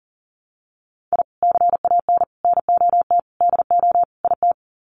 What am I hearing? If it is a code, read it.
Morse code, heard as a keyed tone, about 40 wpm, 700 Hz. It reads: I CAN NOT DO IT
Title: Sets of 5 Words Encore 40wpm